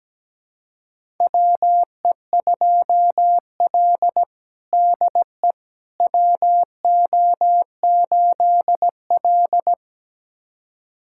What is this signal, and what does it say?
Morse code, heard as a keyed tone, about 17 wpm, 700 Hz. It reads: WE2L DE WO8L
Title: USA Callsign DE Callsign - 4 Characters - 17wpm